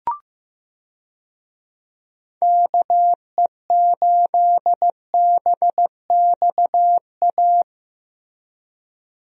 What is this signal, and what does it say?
Morse code, heard as a keyed tone, about 16 wpm, 700 Hz. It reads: KE8BXA
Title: USA Call Signs - 6 Characters - Repeated 3x - 15wpm